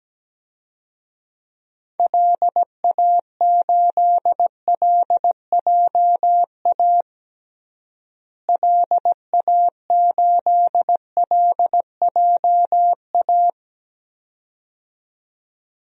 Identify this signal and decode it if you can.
Morse code, heard as a keyed tone, about 17 wpm, 700 Hz. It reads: LA8LJA LA8LJA
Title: DXCC-100 Call Signs - 6 Characters - Repeated 2x - 17wpm